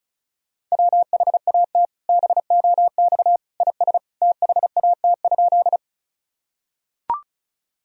Morse code, heard as a keyed tone, about 35 wpm, 700 Hz. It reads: WHAT BOX IS THAT?
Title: Sentences from Top 1000 Words - Review Part 2 - 35wpm